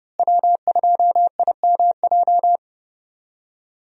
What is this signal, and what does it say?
Morse code, heard as a keyed tone, about 30 wpm, 700 Hz. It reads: W2IMJ
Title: USA Call Signs - 5 Characters - Repeated 3x - 30wpm